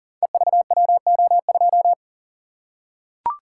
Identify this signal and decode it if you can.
Morse code, heard as a keyed tone, about 41 wpm, 700 Hz. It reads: EVWO2
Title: Code Groups - 5 Letter:Numbers - 40wpm